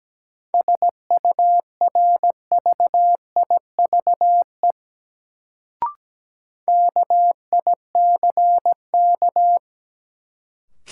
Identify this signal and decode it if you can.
Morse code, heard as a keyed tone, about 17 wpm, 700 Hz. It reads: SURVIVE KICK
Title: Sentences from Top 1300 Words - New Word Drill - 17wpm